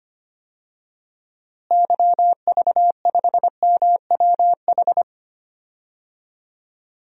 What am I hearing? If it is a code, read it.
Morse code, heard as a keyed tone, about 25 wpm, 700 Hz. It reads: YV5MWH
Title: DXCC-100 Call Signs - 6 Characters - Repeated 2x - 25wpm